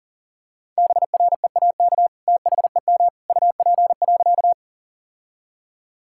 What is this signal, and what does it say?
Morse code, heard as a keyed tone, about 37 wpm, 700 Hz. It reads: BREAK THEM UP.
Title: Sentences from Top 1000 Words - Review Part 6 - 40wpm